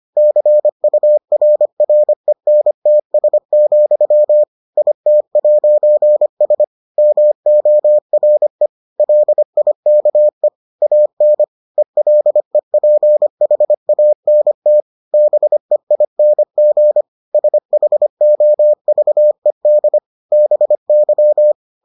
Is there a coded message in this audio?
Yes: Morse code from 0.2 to 21.5 s, heard as a keyed tone, about 25 wpm, 600 Hz. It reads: CURRENTS, IT'S MORE LIKE AN ELEPHANT BEING SHOVED BY